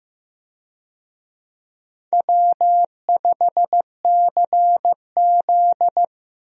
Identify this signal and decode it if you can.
Morse code, heard as a keyed tone, about 15 wpm, 700 Hz. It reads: W5CZ
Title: USA Call Signs - 4 Characters - Repeated 2x - 15wpm